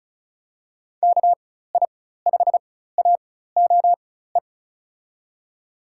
Morse code, heard as a keyed tone, about 36 wpm, spaced as for 16 wpm, 700 Hz. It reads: KI5AOE
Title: USA Call Signs - 4x Character Spacing - 35wpm